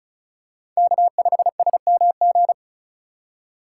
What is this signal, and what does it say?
Morse code, heard as a keyed tone, about 36 wpm, 700 Hz. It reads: K5SMG
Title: USA Call Signs - 5 Characters - 3x Character Spacing - 35wpm